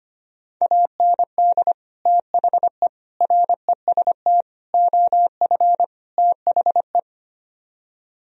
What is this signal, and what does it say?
Morse code, heard as a keyed tone, about 33 wpm, 700 Hz. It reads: AND THE REST OF THE